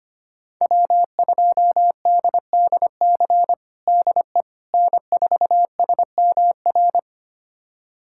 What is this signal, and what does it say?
Morse code, heard as a keyed tone, about 25 wpm, 700 Hz. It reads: W2DDC DE N4SMR
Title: USA Callsign DE Callsign - 5 Characters - 25wpm